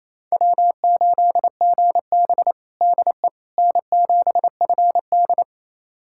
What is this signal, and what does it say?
Morse code, heard as a keyed tone, about 28 wpm, 700 Hz. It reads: W8GB DE N7FD
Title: USA Callsign DE Callsign - 4 Characters - 28wpm